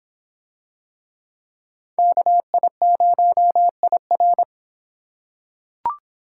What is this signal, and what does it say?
Morse code, heard as a keyed tone, about 26 wpm, 700 Hz. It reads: KI0IR